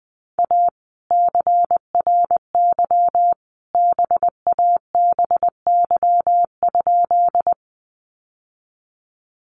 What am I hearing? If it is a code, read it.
Morse code, heard as a keyed tone, about 20 wpm, 700 Hz. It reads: A CRY BABY?
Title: Sentences from Top 1300 Words 20wpm